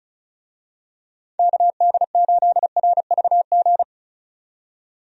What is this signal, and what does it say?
Morse code, heard as a keyed tone, about 36 wpm, 700 Hz. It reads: KD8RVG
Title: USA Call Signs - 6 Characters - 35wpm